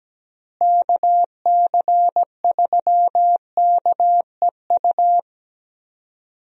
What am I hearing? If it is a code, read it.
Morse code, heard as a keyed tone, about 17 wpm, 700 Hz. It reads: KC3KEU
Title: USA Call Signs - 6 Characters - Repeated 3x - 17wpm